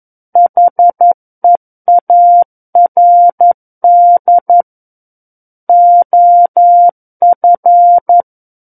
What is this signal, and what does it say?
Morse code, heard as a keyed tone, about 11 wpm, 700 Hz. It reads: HEARD OF
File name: 11 WPM